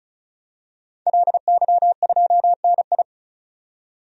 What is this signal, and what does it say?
Morse code, heard as a keyed tone, about 36 wpm, 700 Hz. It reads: LY2NI
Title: DXCC-100 Call Signs - 5 Characters - 35wpm